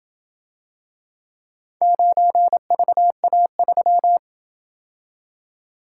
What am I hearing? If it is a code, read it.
Morse code, heard as a keyed tone, about 28 wpm, 700 Hz. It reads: 9VA3